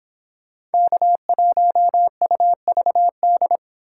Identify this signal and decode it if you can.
Morse code, heard as a keyed tone, about 31 wpm, 700 Hz. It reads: K1UVD